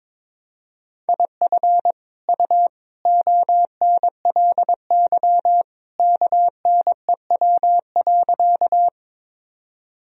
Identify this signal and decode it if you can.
Morse code, heard as a keyed tone, about 22 wpm, 700 Hz. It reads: IF U ONLY KNEW.
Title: Sentences from Top 400 Words and Abbreviations 22wpm